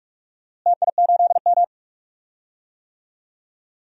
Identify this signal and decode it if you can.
Morse code, heard as a keyed tone, about 46 wpm, 700 Hz. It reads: TI8M
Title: DXCC-100 Call Signs - 4 Characters - Repeated 2x - 45wpm